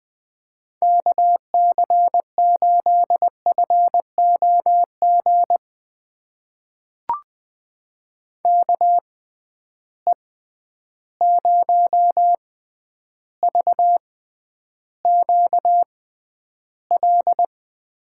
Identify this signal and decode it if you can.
Morse code, heard as a keyed tone, about 20 wpm, spaced as for 10 wpm, 700 Hz. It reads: KC8FOG KE0VQL
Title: USA Call Signs - 6 Characters - 6x Character Spacing - 20wpm